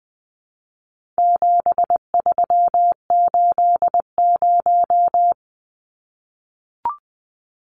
Morse code, heard as a keyed tone, about 20 wpm, 700 Hz. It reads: 7380
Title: Numbers - 4-Digit 20wpm